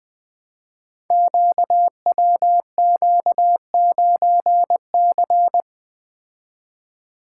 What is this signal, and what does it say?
Morse code, heard as a keyed tone, about 21 wpm, 700 Hz. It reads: QWQ9C
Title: Code Groups - 5 Letter:Numbers - 20wpm